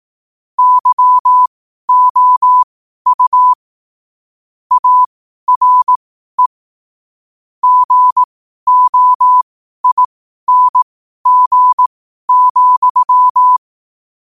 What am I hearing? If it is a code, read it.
Morse code, heard as a keyed tone, about 18 wpm, spaced as for 12 wpm, 1000 Hz. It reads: YOU ARE GOING,
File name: qotdfiles / 12wpm